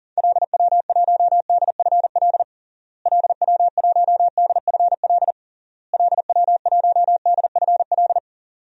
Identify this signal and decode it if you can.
Morse code, heard as a keyed tone, about 40 wpm, 700 Hz. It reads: LW1DFL LW1DFL LW1DFL
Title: DXCC-100 Call Signs - 6 Characters - Repeated 3x - 40wpm